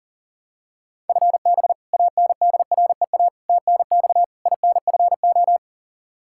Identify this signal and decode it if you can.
Morse code, heard as a keyed tone, about 40 wpm, 700 Hz. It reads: FB ANDREA TNX INFO
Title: Practice QSO - Exchange 3 - More Info - 40wpm